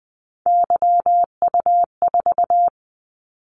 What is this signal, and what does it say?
Morse code, heard as a keyed tone, about 20 wpm, 700 Hz. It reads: YU4
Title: Callsign Prefixes for Top 25 DXCC - 20wpm